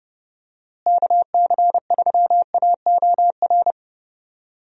Morse code, heard as a keyed tone, about 32 wpm, 700 Hz. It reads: KC3AOR
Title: USA Call Signs - 6 Characters - Repeated 2x - 30wpm